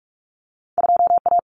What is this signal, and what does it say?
Morse code, heard as a keyed tone, about 46 wpm, 700 Hz. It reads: 2A
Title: Common Cut Numbers - 2-Digit 45wpm